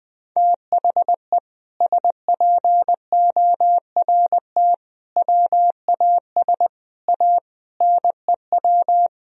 Morse code, heard as a keyed tone, about 21 wpm, 700 Hz. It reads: THE SPORT WAS A NEW
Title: Sentences from Top 1000 Words - Review Part 4 - 20wpm